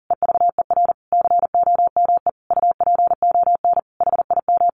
Morse code, heard as a keyed tone, about 40 wpm, 700 Hz. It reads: EVER COME UPON HIM
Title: Sentences from Top 1100 Words 40wpm